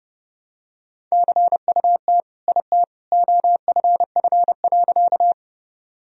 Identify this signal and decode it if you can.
Morse code, heard as a keyed tone, about 32 wpm, 700 Hz. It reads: CUT IT OFF.
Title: Sentences from Top 1000 Words - Review Part 1 - 30wpm